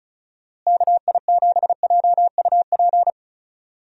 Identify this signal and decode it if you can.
Morse code, heard as a keyed tone, about 36 wpm, 700 Hz. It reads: KI7JUP
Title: USA Call Signs - 6 Characters - Repeated 2x - 35wpm